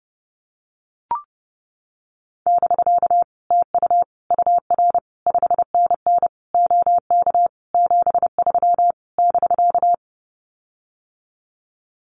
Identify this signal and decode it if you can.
Morse code, heard as a keyed tone, about 30 wpm, 700 Hz. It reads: <BK> TU UR 5NN OK 73 <BK>
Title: Common QSO Phrases Encore 30wpm